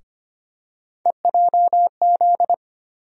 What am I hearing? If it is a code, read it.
Morse code, heard as a keyed tone, about 25 wpm, 700 Hz. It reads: EJZ
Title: USA Call Sign Postfixes - 3 Letters 25wpm